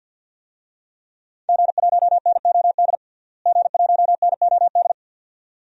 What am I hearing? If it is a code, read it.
Morse code, heard as a keyed tone, about 48 wpm, 700 Hz. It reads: G1NOD G1NOD
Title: DXCC-100 Call Signs - 5 Characters - Repeated 2x - 50wpm